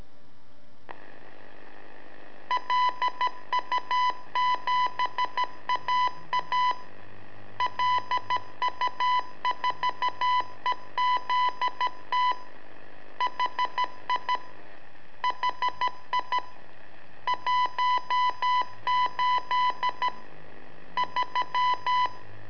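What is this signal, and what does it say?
Morse code, heard as a keyed tone, about 13 wpm, 1000 Hz. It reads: LU7AA LU4EZT HI HI 18 3
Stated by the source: Telemetry in Morse from LU7AA and LU4EZ, AMSAT and Turma Mar del Plata, sent from a balloon.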